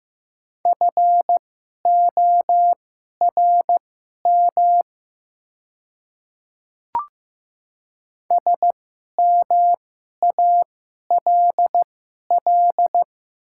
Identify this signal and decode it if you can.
Morse code, heard as a keyed tone, about 15 wpm, spaced as for 11 wpm, 700 Hz. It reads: FORM SMALL
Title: 2x Character Spacing - Top 300 Words - 15wpm